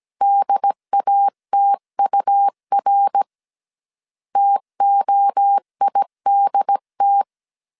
Morse code, heard as a keyed tone, about 17 wpm, 800 Hz. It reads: DATUR TOIDT
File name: CW / DET / 1 3 1500C S A 16wpm 2141